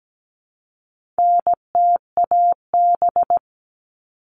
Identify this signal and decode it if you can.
Morse code, heard as a keyed tone, about 17 wpm, 700 Hz. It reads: NTAB
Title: All Cut Numbers - 4-Digit 17wpm